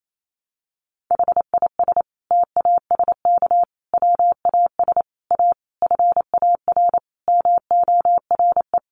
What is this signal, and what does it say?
Morse code, heard as a keyed tone, about 28 wpm, 700 Hz. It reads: HIS TASK WAS A FAR MORE
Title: Sentences from Top 1200 Words 28wpm